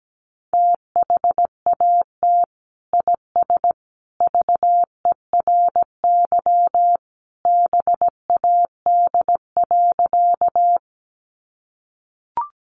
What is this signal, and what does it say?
Morse code, heard as a keyed tone, about 17 wpm, 700 Hz. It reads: THAT IS VERY BAD.
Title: Sentences from Top 700 Words 17wpm